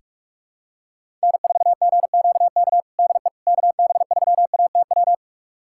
Morse code, heard as a keyed tone, about 41 wpm, 700 Hz. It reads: N4GQK DE KB3ATW
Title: USA Callsign DE Callsign 45wpm